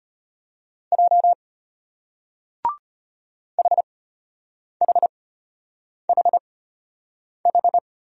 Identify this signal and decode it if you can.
Morse code, heard as a keyed tone, about 34 wpm, spaced as for 8 wpm, 700 Hz. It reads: J HHHH